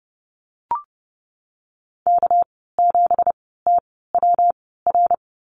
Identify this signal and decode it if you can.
Morse code, heard as a keyed tone, about 30 wpm, spaced as for 17 wpm, 700 Hz. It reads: K7TWR
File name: Lesson 37g - Letter x - Call Signs 30wpm